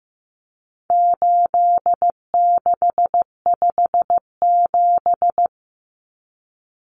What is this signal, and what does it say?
Morse code, heard as a keyed tone, about 15 wpm, 700 Hz. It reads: 8657
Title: Common Cut Numbers - 4-Digit 15wpm